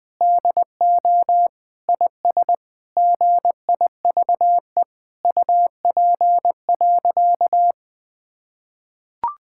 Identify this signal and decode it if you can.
Morse code, heard as a keyed tone, about 20 wpm, 700 Hz. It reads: DO IS GIVE UP.
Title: Sentences with 7-Word Phrases - 20wpm